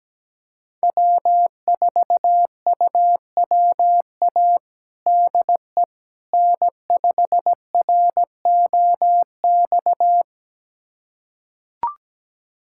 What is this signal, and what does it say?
Morse code, heard as a keyed tone, about 17 wpm, 700 Hz. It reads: W4UWA DE N5ROX
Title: USA Callsign DE Callsign - 5 Characters - 17wpm